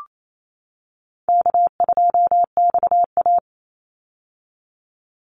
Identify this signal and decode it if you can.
Morse code, heard as a keyed tone, about 28 wpm, 700 Hz. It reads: K2XA
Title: Easy Call Signs - 28wpm